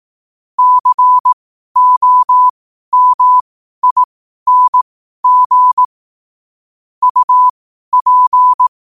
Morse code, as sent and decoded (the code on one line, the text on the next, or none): -.-. --- -- .. -. --. / ..- .--.
COMING UP